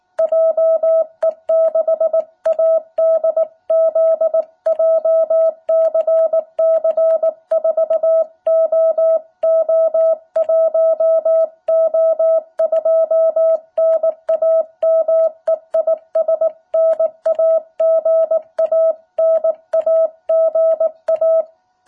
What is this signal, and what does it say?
Morse code, heard as a keyed tone, about 19 wpm, 660 Hz. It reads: JE6ADZ JCC4OO1O2 NAME IS NAGA NAGA
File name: jh8cbh / 2021cw